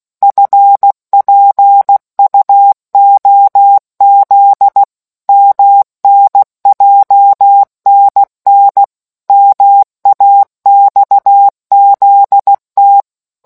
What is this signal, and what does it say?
Morse code, heard as a keyed tone, about 16 wpm, 800 Hz. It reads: FPUOZ MNJNN MAXZT